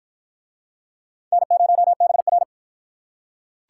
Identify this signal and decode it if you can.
Morse code, heard as a keyed tone, about 46 wpm, 700 Hz. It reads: N0BR